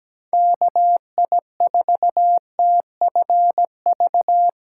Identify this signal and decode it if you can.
Morse code, heard as a keyed tone, about 17 wpm, 700 Hz. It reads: KI4TFV
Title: USA Call Signs - 6 Characters - 3x Character Spacing - 17wpm